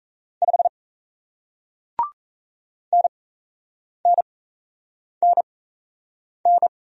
Morse code, heard as a keyed tone, about 37 wpm, spaced as for 8 wpm, 700 Hz. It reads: 5 NNNN